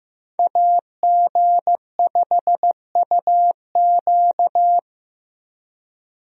Morse code, heard as a keyed tone, about 15 wpm, 700 Hz. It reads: AG5UQ
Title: USA Call Signs - 5 Characters - Repeated 3x - 15wpm